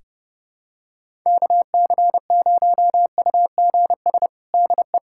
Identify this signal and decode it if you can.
Morse code, heard as a keyed tone, about 30 wpm, 700 Hz. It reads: KC0UGS DE
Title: USA Callsign DE Callsign - 6 Characters - 30wpm